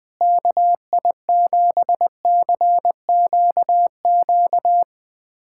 Morse code, heard as a keyed tone, about 21 wpm, 700 Hz. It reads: KI7CQQ
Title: USA Callsign DE Callsign 20wpm